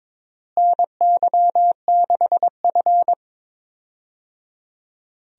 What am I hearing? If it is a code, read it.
Morse code, heard as a keyed tone, about 22 wpm, 700 Hz. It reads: NY6F
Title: USA Call Signs - 4 Characters - Repeated 3x - 22wpm